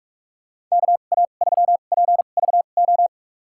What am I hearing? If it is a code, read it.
Morse code, heard as a keyed tone, about 46 wpm, spaced as for 33 wpm, 700 Hz. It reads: KA3PVO
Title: USA Call Signs - 2x Character Spacing - 45wpm